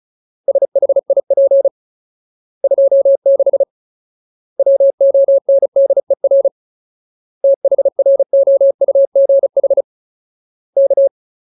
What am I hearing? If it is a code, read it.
Morse code, heard as a keyed tone, about 36 wpm, 550 Hz. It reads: SHIP 26 WONDER THROUGH K
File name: Ultimate Word List / 35WPM Ultimate